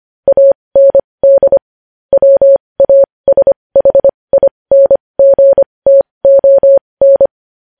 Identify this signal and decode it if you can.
Morse code, heard as a keyed tone, about 25 wpm, spaced as for 20 wpm, 550 Hz. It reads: AND WASHINGTON
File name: podcast / News headlines-1x-25x20